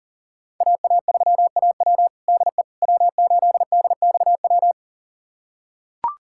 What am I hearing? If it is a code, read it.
Morse code, heard as a keyed tone, about 40 wpm, 700 Hz. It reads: AA3AW DE W8DXW
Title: USA Callsign DE Callsign - 5 Characters - 40wpm